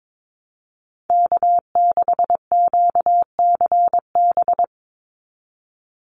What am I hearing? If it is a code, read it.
Morse code, heard as a keyed tone, about 22 wpm, 700 Hz. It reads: K6QCB
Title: NA SOTA Chaser Callsigns 22wpm